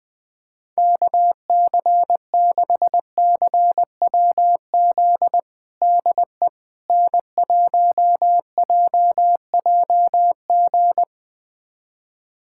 Morse code, heard as a keyed tone, about 20 wpm, 700 Hz. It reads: KC6CWZ DE N1JJG
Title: USA Callsign DE Callsign 20wpm